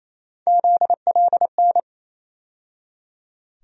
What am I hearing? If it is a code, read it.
Morse code, heard as a keyed tone, about 28 wpm, 700 Hz. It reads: ZLN
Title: USA Call Sign Postfixes - 3 Letters 28wpm